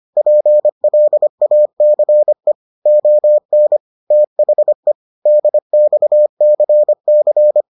Morse code, heard as a keyed tone, about 25 wpm, 600 Hz. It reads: PLACE ON THE DXCC